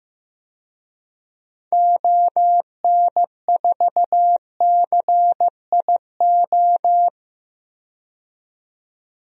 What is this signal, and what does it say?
Morse code, heard as a keyed tone, about 15 wpm, 700 Hz. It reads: ON4CIO
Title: DXCC-100 Call Signs - 6 Characters - 15wpm